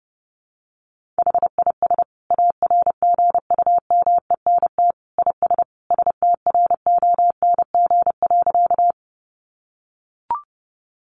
Morse code, heard as a keyed tone, about 28 wpm, 700 Hz. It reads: HIS ARGUMENT IS STRONG.
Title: Sentences from Top 1400 Words 30wpm